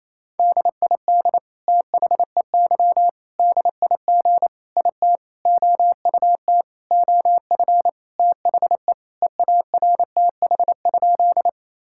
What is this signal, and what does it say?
Morse code, heard as a keyed tone, about 28 wpm, 700 Hz. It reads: DID THEY DIG IT OUT OF THE EARTH?
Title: Sentences from Top 1600 Words 28wpm